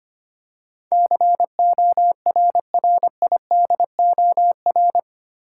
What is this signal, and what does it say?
Morse code, heard as a keyed tone, about 25 wpm, 700 Hz. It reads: CORRIDOR
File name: 8 Letter Words - First 3 Letters Given - 25wpm